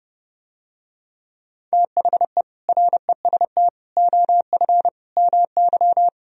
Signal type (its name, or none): Morse code